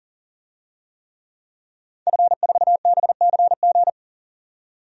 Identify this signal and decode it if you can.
Morse code, heard as a keyed tone, about 40 wpm, 700 Hz. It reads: F4BCG
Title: DXCC-100 Call Signs - 5 Characters - 40wpm